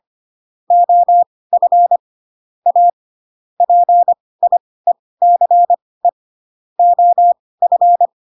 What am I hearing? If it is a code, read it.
Morse code, heard as a keyed tone, about 25 wpm, spaced as for 18 wpm, 700 Hz. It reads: OF A PIECE OF
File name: Pinocchio / Pinocchio ch5 12 25 ik1hge